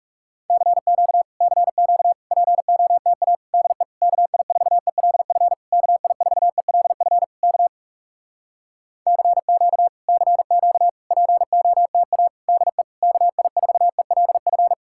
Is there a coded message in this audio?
Yes: Morse code from 0.5 to 14.7 s, heard as a keyed tone, about 40 wpm, 700 Hz. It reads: CQ CQ POTA DE KI4ELF KI4ELF K CQ CQ POTA DE KI4ELF